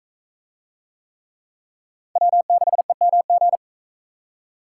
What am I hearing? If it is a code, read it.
Morse code, heard as a keyed tone, about 42 wpm, 700 Hz. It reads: W6EMG